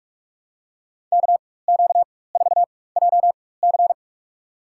Morse code, heard as a keyed tone, about 47 wpm, spaced as for 21 wpm, 700 Hz. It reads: KQ4JC
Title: USA Call Signs - 4x Character Spacing - 45wpm